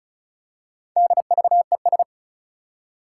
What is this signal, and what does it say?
Morse code, heard as a keyed tone, about 36 wpm, 700 Hz. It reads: DVES
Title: Four Challenging Letter-Number Combinations 35wpm